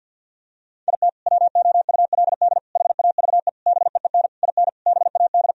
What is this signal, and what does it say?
Morse code, heard as a keyed tone, about 52 wpm, 700 Hz. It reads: IT WOULD HAVE BEEN IN BAD
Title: Sentences from Top 1000 Words - Review Part 1 - 50wpm